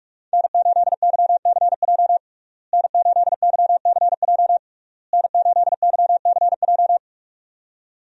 Morse code, heard as a keyed tone, about 41 wpm, 700 Hz. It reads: N8YCJ N8YCJ N8YCJ
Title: USA Call Signs - 5 Characters - Repeated 3x - 45wpm